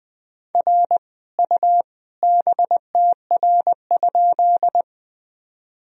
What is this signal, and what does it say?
Morse code, heard as a keyed tone, about 21 wpm, 700 Hz. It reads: R U BTR?
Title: Sentences from Top 400 Words and Abbreviations 20wpm